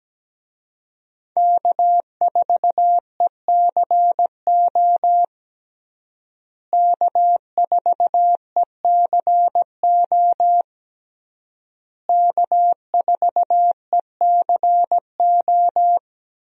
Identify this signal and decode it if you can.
Morse code, heard as a keyed tone, about 17 wpm, 700 Hz. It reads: K4ECO K4ECO K4ECO
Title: USA Call Signs - 5 Characters - Repeated 3x - 17wpm